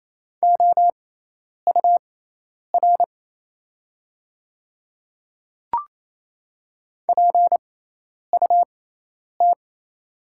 Morse code, heard as a keyed tone, about 28 wpm, spaced as for 10 wpm, 700 Hz. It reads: OUR PUT
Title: 6x Character Spacing - Top 500 Words - 28wpm